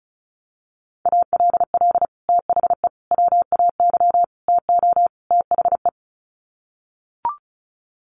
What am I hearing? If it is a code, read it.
Morse code, heard as a keyed tone, about 34 wpm, 700 Hz. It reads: ALL THE WAY TO THE
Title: Sets of 5 Words 35wpm